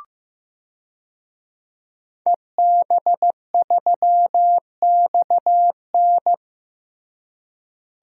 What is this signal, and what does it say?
Morse code, heard as a keyed tone, about 15 wpm, 700 Hz. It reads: EB3XN
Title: DXCC-100 Call Signs - 5 Characters - Repeated 3x - 15wpm